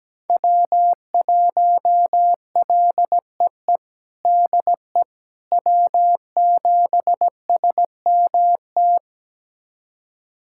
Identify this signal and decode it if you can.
Morse code, heard as a keyed tone, about 17 wpm, 700 Hz. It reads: W1LEE DE W7SMT
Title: USA Callsign DE Callsign - 5 Characters - 17wpm